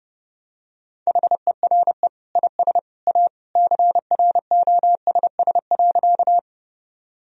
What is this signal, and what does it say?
Morse code, heard as a keyed tone, about 32 wpm, 700 Hz. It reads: HERE IS A CROSS.
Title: Sentences from Top 1000 Words - Review Part 1 - 30wpm